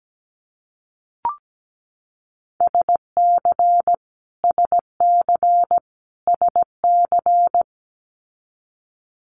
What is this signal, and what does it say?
Morse code, heard as a keyed tone, about 17 wpm, 700 Hz. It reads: SC SC SC
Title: Multi-letter Phonemes Repeated 3x 17wpm